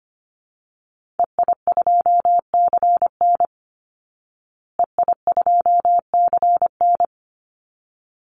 Morse code, heard as a keyed tone, about 25 wpm, 700 Hz. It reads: EI2CN EI2CN
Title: Call Signs Repeated 3x - 25wpm